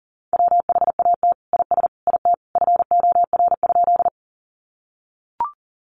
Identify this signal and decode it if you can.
Morse code, heard as a keyed tone, about 40 wpm, 700 Hz. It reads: WHAT IS IT FOR?
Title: Sentences from Top 100 Words 40wpm